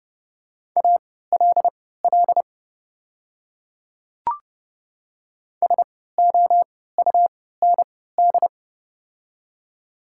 Morse code, heard as a keyed tone, about 30 wpm, spaced as for 17 wpm, 700 Hz. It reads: ALL SOUND
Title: 3x Character Spacing - Top 100 Words - 30wpm